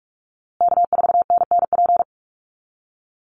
Morse code, heard as a keyed tone, about 44 wpm, 700 Hz. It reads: K4NNP
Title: USA Call Signs Repeated 3x 45wpm